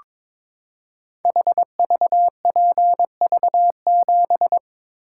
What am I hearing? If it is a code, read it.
Morse code, heard as a keyed tone, about 22 wpm, 700 Hz. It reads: HVPV7
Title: Code Groups - 5 Letter:Numbers - 22wpm